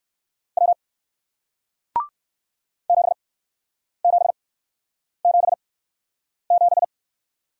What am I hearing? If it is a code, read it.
Morse code, heard as a keyed tone, about 58 wpm, spaced as for 9 wpm, 700 Hz. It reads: W 7777